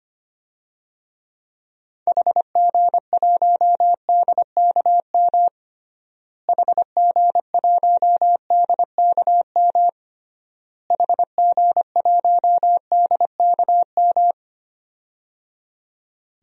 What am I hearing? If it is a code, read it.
Morse code, heard as a keyed tone, about 25 wpm, 700 Hz. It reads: HG1DKM HG1DKM HG1DKM
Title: DXCC-100 Call Signs - 6 Characters - Repeated 3x - 25wpm